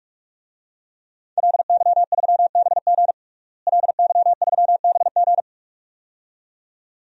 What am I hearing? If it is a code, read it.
Morse code, heard as a keyed tone, about 46 wpm, 700 Hz. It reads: LY3BG LY3BG
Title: DXCC-100 Call Signs - 5 Characters - Repeated 2x - 45wpm